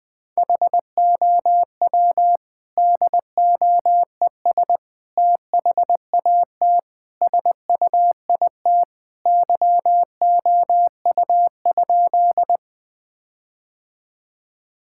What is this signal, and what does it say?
Morse code, heard as a keyed tone, about 21 wpm, 700 Hz. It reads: HOW DOES THAT SUIT YOU?
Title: Sentences from Top 1600 Words 20wpm